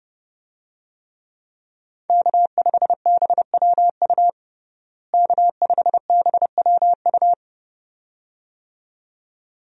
Morse code, heard as a keyed tone, about 32 wpm, 700 Hz. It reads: K5BWU K5BWU
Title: USA Call Signs - 5 Characters - Repeated 2x - 30wpm